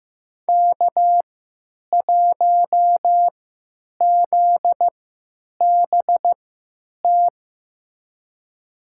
Morse code, heard as a keyed tone, about 15 wpm, spaced as for 9 wpm, 700 Hz. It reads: K1ZBT
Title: USA Call Signs - 5 Characters - 3x Character Spacing - 15wpm